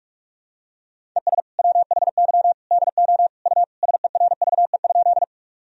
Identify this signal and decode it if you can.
Morse code, heard as a keyed tone, about 45 wpm, 700 Hz. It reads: ES WHY DO U SERVE?
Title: Sentences from Top 400 Words and Abbreviations 45wpm